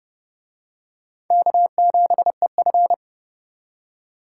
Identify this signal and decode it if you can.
Morse code, heard as a keyed tone, about 27 wpm, 700 Hz. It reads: K7EF